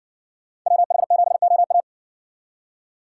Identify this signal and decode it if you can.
Morse code, heard as a keyed tone, about 60 wpm, 700 Hz. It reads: WH7OA